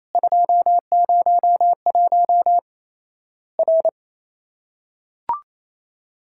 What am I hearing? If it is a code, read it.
Morse code, heard as a keyed tone, about 28 wpm, 700 Hz. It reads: 201 R
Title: Worked All Europe DX Contest - QTC Exchange - 28wpm